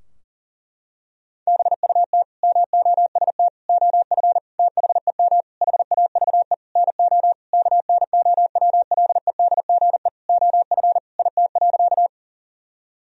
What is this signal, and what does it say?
Morse code, heard as a keyed tone, about 40 wpm, 700 Hz. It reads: BUT MOST OF THEM HAVE NO KNOWLEDGE OF IT.
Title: Sentences from Top 1000 Words - Review Part 3 - 40wpm